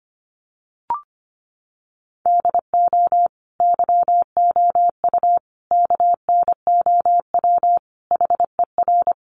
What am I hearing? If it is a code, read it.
Morse code, heard as a keyed tone, about 25 wpm, 700 Hz. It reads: DO YOU KNOW HER
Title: Sentences from Top 1400 Words 25wpm